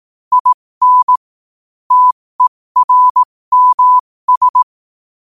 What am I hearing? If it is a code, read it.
Morse code, heard as a keyed tone, about 18 wpm, spaced as for 15 wpm, 1000 Hz. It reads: IN TERMS
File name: qotdfiles / 15wpm